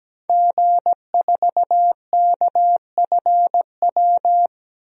Morse code, heard as a keyed tone, about 17 wpm, 700 Hz. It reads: G4KFW
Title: DXCC-100 Call Signs - 5 Characters - Repeated 2x - 17wpm